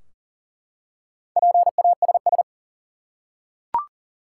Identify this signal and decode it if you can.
Morse code, heard as a keyed tone, about 37 wpm, 700 Hz. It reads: PASS
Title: Four Letter Words - 40wpm